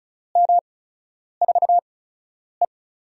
Morse code, heard as a keyed tone, about 35 wpm, spaced as for 10 wpm, 700 Hz. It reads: M4E
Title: USA Call Signs - 4 Characters - 8x Character Spacing - 35wpm